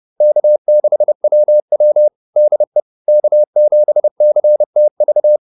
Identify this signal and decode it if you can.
Morse code, heard as a keyed tone, about 30 wpm, 600 Hz. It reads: K6WW DE K7CTV